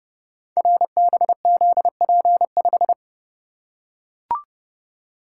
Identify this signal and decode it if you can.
Morse code, heard as a keyed tone, about 30 wpm, 700 Hz. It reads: RBZP5
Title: Code Groups - 5 Letter:Numbers - 30wpm